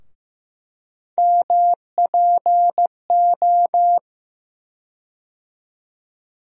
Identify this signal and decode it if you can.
Morse code, heard as a keyed tone, about 15 wpm, 700 Hz. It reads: MPO
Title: USA Call Sign Postfixes - 3 Letters 15wpm